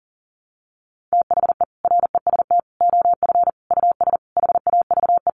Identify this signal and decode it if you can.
Morse code, heard as a keyed tone, about 40 wpm, 700 Hz. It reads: THE REST OF US HAVE
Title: Sentences from Top 700 Words 40wpm